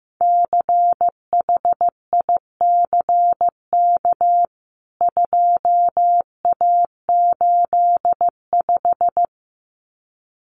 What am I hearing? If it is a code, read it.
Morse code, heard as a keyed tone, about 15 wpm, 700 Hz. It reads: CHICK 2A85
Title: CWT Exchange 15wpm